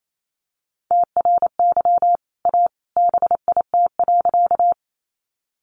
Morse code, heard as a keyed tone, about 28 wpm, 690 Hz. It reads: TRY A BIT.
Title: Sentences from Top 900 Words 28wpm